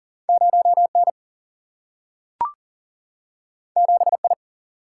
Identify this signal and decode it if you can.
Morse code, heard as a keyed tone, about 40 wpm, 700 Hz. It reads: 0 N 7 I
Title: Single-Letter Single-Number 40wpm